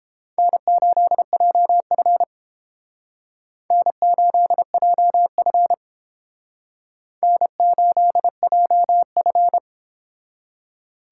Morse code, heard as a keyed tone, about 30 wpm, 700 Hz. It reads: N8JF N8JF N8JF